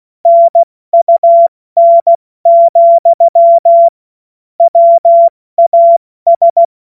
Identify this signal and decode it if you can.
Morse code, heard as a keyed tone, about 16 wpm, 680 Hz. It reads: NUN, WAS
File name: Morsekurs / Witze / Witz 10. Mai 2022 70 F80